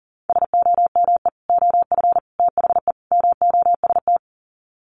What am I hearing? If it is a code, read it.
Morse code, heard as a keyed tone, about 38 wpm, 700 Hz. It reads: SOME OF THE MOST
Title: Sets of 4 Words 40wpm